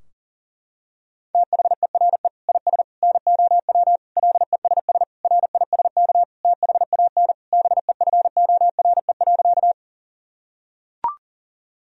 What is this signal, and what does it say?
Morse code, heard as a keyed tone, about 38 wpm, 700 Hz. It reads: THERE IS NOW LESS RISK THAN BEFORE.
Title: Sentences from Top 1000 Words - Review Part 4 - 40wpm